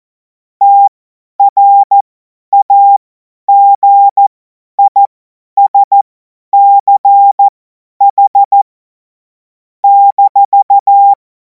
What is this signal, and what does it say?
Morse code, heard as a keyed tone, about 14 wpm, spaced as for 10 wpm, 800 Hz. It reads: TRAGISCH -
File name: Morsekurs / Witze / Witz 05. Mai 2026 50 F70